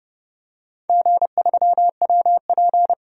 Morse code, heard as a keyed tone, about 30 wpm, 700 Hz. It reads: G3WP
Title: DXCC-100 Call Signs - 4 Characters - 30wpm